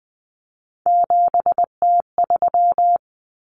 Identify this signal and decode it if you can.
Morse code, heard as a keyed tone, about 20 wpm, 700 Hz. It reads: 7T3
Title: Common Cut Numbers - 3-Digit 20wpm